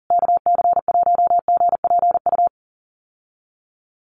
Morse code, heard as a keyed tone, about 40 wpm, 700 Hz. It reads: KC1GPU
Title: USA Call Signs Repeated 2x 40wpm